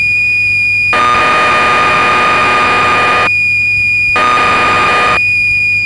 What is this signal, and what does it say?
Cassette data (Kansas City type).
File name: magazines / Acorn User / 1984-09 BBC Programs